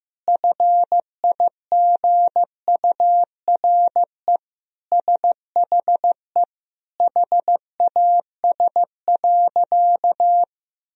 Morse code, heard as a keyed tone, about 15 wpm, 700 Hz. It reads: FIGURE SHE HAS.
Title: Sentences from Top 1000 Words - Review Part 2 - 15wpm